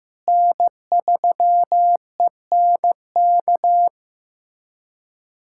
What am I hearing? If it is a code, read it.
Morse code, heard as a keyed tone, about 15 wpm, 700 Hz. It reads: N3ENK
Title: USA Callsign DE Callsign - 5 Characters - 15wpm